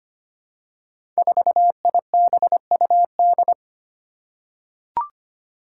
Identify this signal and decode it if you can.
Morse code, heard as a keyed tone, about 25 wpm, 700 Hz. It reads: 4IBUD
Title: Five Challenging Letter-Number Combinations 25wpm